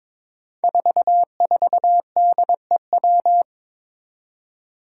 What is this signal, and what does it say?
Morse code, heard as a keyed tone, about 22 wpm, 700 Hz. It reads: 44DEW
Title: Five Challenging Letter-Number Combinations 22wpm